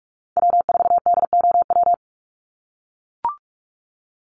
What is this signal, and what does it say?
Morse code, heard as a keyed tone, about 44 wpm, 690 Hz. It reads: W4DOW
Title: NA SOTA Chaser Callsigns 45wpm